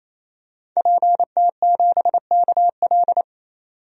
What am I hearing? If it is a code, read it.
Morse code, heard as a keyed tone, about 28 wpm, 700 Hz. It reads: PT7KL
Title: DXCC-100 Call Signs - 5 Characters - 28wpm